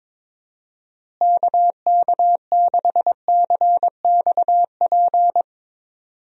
Morse code, heard as a keyed tone, about 22 wpm, 700 Hz. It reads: KK6CXP
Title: USA Call Signs - 6 Characters - Repeated 2x - 22wpm